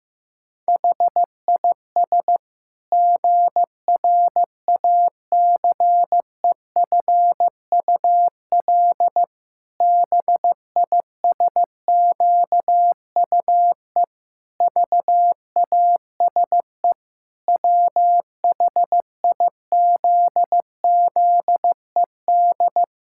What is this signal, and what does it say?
Morse code, heard as a keyed tone, about 15 wpm, 700 Hz. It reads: HIS GRACEFUL BISQUE VASE WHIZZED
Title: Pangram Challenge v2 - 15wpm